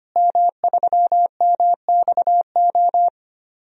Morse code, heard as a keyed tone, about 25 wpm, 700 Hz. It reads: M3MXO
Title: DXCC-100 Call Signs - 5 Characters - Repeated 2x - 25wpm